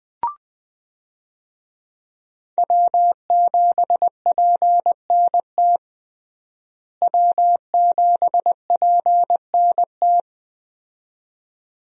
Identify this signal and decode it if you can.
Morse code, heard as a keyed tone, about 20 wpm, 700 Hz. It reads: W7PNT W7PNT
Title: USA Call Signs - 5 Characters - Repeated 2x - 20wpm